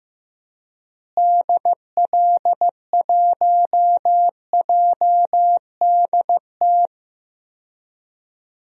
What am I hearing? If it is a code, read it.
Morse code, heard as a keyed tone, about 15 wpm, 700 Hz. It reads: DL1JDT
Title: DXCC-100 Call Signs - 6 Characters - Repeated 3x - 15wpm